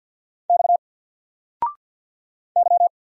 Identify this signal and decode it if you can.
Morse code, heard as a keyed tone, about 48 wpm, 700 Hz. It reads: X Y
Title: Single Character (Letter:Number:Punctuation:Prosign) 50wpm